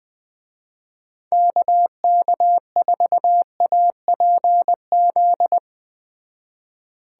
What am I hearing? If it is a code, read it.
Morse code, heard as a keyed tone, about 20 wpm, 700 Hz. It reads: KK4APZ
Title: Speed Racing - USA 2x3 Call Signs 17wpm